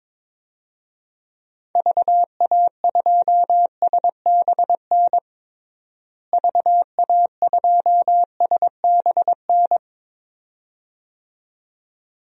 Morse code, heard as a keyed tone, about 22 wpm, 700 Hz. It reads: VA2SBN VA2SBN
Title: DXCC-100 Call Signs - 6 Characters - Repeated 2x - 22wpm